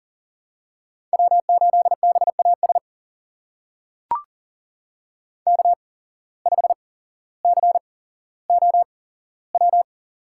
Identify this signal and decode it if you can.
Morse code, heard as a keyed tone, about 41 wpm, spaced as for 17 wpm, 700 Hz. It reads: W8BAS K5COW
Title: USA Call Signs - 5 Characters - 8x Character Spacing - 40wpm